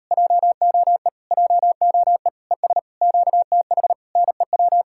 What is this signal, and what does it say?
Morse code, heard as a keyed tone, about 38 wpm, 700 Hz. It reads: JOE JOE ES QTH NEW